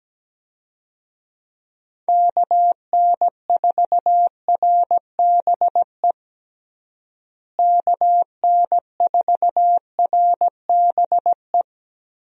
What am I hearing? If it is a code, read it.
Morse code, heard as a keyed tone, about 17 wpm, 700 Hz. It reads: KN4RBE KN4RBE
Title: USA Call Signs - 6 Characters - Repeated 3x - 17wpm